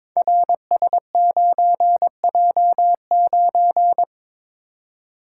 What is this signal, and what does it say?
Morse code, heard as a keyed tone, about 22 wpm, 700 Hz. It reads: RS9J9
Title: Code Groups - 5 Letter:Numbers - 22wpm